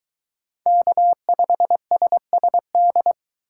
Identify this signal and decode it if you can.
Morse code, heard as a keyed tone, about 22 wpm, 700 Hz. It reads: K5SSD